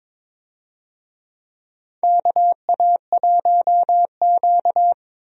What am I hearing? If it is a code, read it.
Morse code, heard as a keyed tone, about 22 wpm, 700 Hz. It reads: KA1Q
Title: USA Call Signs - 4 Characters - Repeated 2x - 22wpm